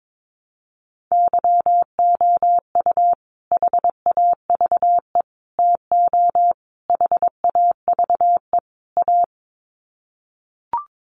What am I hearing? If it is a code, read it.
Morse code, heard as a keyed tone, about 22 wpm, 700 Hz. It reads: YOU HAVE TO HAVE A
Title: Sets of 5 Words Encore 22wpm